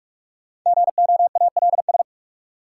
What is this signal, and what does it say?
Morse code, heard as a keyed tone, about 43 wpm, 700 Hz. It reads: GOALS
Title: Sentences from Top 1000 Words - Review Part 6 - New Word Drill - 45wpm